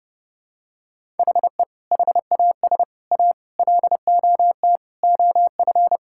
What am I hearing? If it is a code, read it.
Morse code, heard as a keyed tone, about 30 wpm, 700 Hz. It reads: HE HAS A LOT OF
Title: Sentences from Top 1000 Words - Review Part 6 - 30wpm